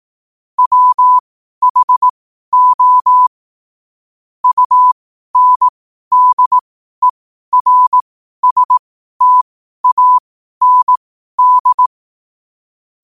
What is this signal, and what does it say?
Morse code, heard as a keyed tone, about 18 wpm, spaced as for 12 wpm, 1000 Hz. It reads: WHO UNDERSTAND
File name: qotdfiles / 12wpm